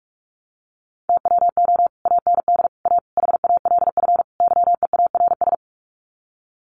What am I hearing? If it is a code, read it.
Morse code, heard as a keyed tone, about 44 wpm, 700 Hz. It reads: TWO AND A HALF YEARS
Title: Sets of 5 Words 45wpm